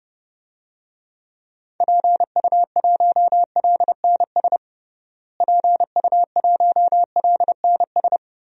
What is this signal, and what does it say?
Morse code, heard as a keyed tone, about 32 wpm, 700 Hz. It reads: PU1LNS PU1LNS
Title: DXCC-100 Call Signs - 6 Characters - Repeated 3x - 30wpm